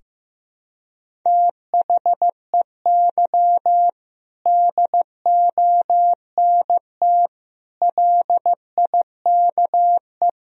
Morse code, heard as a keyed tone, about 15 wpm, 700 Hz. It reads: THEY DONT LIKE
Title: Sentences from Top 1000 Words - Review Part 2 - 15wpm